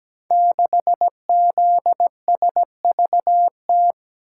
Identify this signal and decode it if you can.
Morse code, heard as a keyed tone, about 17 wpm, 700 Hz. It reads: 6ZSVT
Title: Code Groups - 5 Letter:Numbers - 17wpm